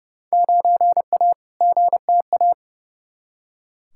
Morse code, heard as a keyed tone, about 31 wpm, 700 Hz. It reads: 9A GTA
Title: Field Day Exchanges Encore - 30wpm